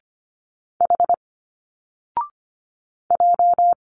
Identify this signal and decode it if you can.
Morse code, heard as a keyed tone, about 26 wpm, 700 Hz. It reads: H J
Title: Single Letters 25wpm